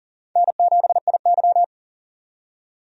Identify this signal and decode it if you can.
Morse code, heard as a keyed tone, about 40 wpm, 700 Hz. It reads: N7IY
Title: USA Call Signs - 4 Characters - 40wpm